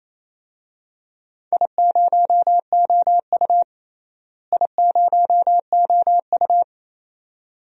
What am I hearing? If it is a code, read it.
Morse code, heard as a keyed tone, about 28 wpm, 700 Hz. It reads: I0OU I0OU
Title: DXCC-100 Call Signs - 4 Characters - Repeated 2x - 28wpm